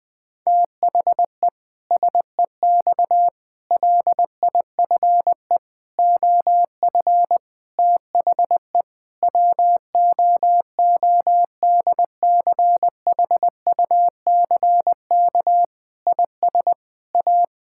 Morse code, heard as a keyed tone, about 20 wpm, 700 Hz. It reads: THE SEX LIFE OF THE WOODCHUCK IS A
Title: Pangram Challenge v2 - 20wpm